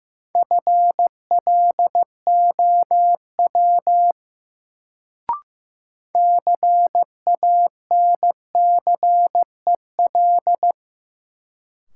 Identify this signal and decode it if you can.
Morse code, heard as a keyed tone, about 15 wpm, 700 Hz. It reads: FLOW CANCEL
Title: Sentences from Top 1200 Words - New Word Drill Alt - 15wpm